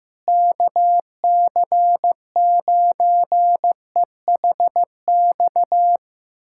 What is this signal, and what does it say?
Morse code, heard as a keyed tone, about 16 wpm, 700 Hz. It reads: KC9EHX
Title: USA Callsign DE Callsign - 6 Characters - 15wpm